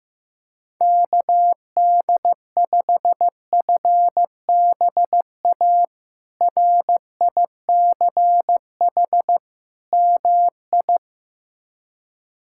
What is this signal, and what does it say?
Morse code, heard as a keyed tone, about 15 wpm, 700 Hz. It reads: KD5FBA RICH MI
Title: K1USN SST QSO Practice - 15wpm